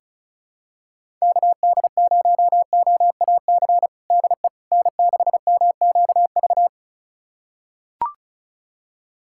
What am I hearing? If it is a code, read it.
Morse code, heard as a keyed tone, about 36 wpm, 700 Hz. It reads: KD0OAC DE N6MQV
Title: USA Callsign DE Callsign 35wpm